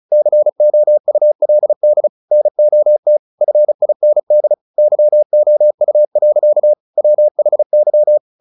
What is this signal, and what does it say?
Morse code, heard as a keyed tone, about 34 wpm, 600 Hz. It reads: COULD NOT FIND YOU. WHY
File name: the old man and the sea 2 35wpm